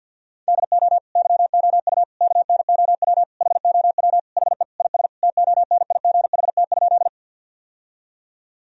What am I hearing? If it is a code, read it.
Morse code, heard as a keyed tone, about 50 wpm, 700 Hz. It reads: DO YOU KNOW HOW HE IS TONIGHT?
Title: Sentences from Top 1000 Words - Review Part 3 - 50wpm